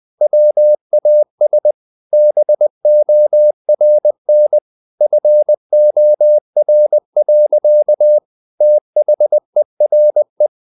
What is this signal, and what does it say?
Morse code, heard as a keyed tone, about 20 wpm, 600 Hz. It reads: WAS BORN FOR. THERE
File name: the old man and the sea / the old man and the sea part 1 20wpm